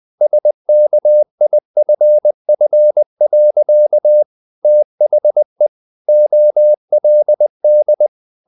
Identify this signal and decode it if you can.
Morse code, heard as a keyed tone, about 20 wpm, 600 Hz. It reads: SKIFF. THE OLD
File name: the old man and the sea part 2 20wpm